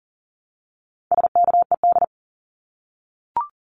Morse code, heard as a keyed tone, about 38 wpm, 700 Hz. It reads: SKED
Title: Abbreviations Repeated 3x - 40wpm